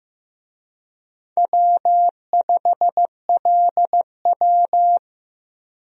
Morse code, heard as a keyed tone, about 16 wpm, 700 Hz. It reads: W5LW
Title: USA Call Signs - 4 Characters - Repeated 3x - 15wpm